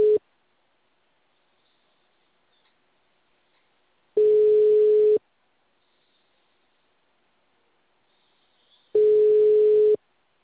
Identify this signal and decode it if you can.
Ringback tone (Europe, 425 Hz).
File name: out- -2011- - - .71934